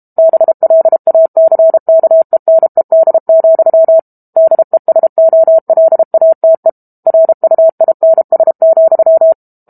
Morse code, heard as a keyed tone, about 32 wpm, 650 Hz. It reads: BLACKENED, DESOLATE RUINS,
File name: Book Two / 06-Chapter 6 - 32 WPM